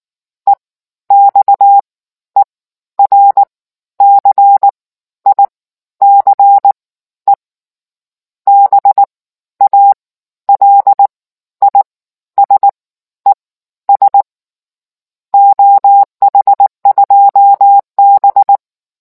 Morse code, heard as a keyed tone, about 19 wpm, spaced as for 12 wpm, 800 Hz. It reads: EXERCICE BALISES OH2B